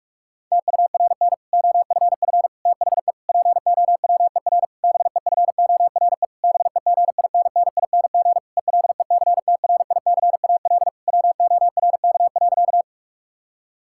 Morse code, heard as a keyed tone, about 41 wpm, 700 Hz. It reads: TURN OFF THE POWER BEFORE BEGINNING ELECTRICAL WORK.
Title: Sentences from Top 1000 Words - Review Part 6 - 45wpm